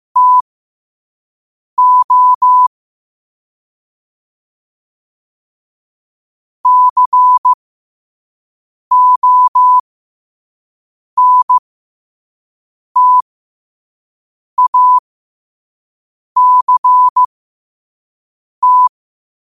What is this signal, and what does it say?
Morse code, heard as a keyed tone, about 15 wpm, spaced as for 5 wpm, 1000 Hz. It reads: TO CONTACT
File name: qotdfiles / 5wpm